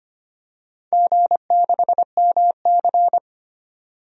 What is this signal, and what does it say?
Morse code, heard as a keyed tone, about 25 wpm, 700 Hz. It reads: G6MC
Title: DXCC-100 Call Signs - 4 Characters - Repeated 2x - 25wpm